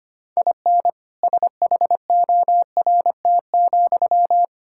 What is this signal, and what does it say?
Morse code, heard as a keyed tone, about 25 wpm, 700 Hz. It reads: IN SHORT,
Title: Sentences from Top 1000 Words - Review Part 5 - 25wpm